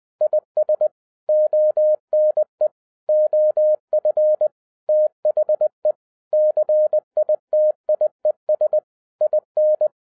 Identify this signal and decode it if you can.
Morse code, heard as a keyed tone, about 20 wpm, 610 Hz. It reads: IS ONE OF THE CITIES IN
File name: 2011-10-mp3 / n03-TARO JR0QWW 20wpm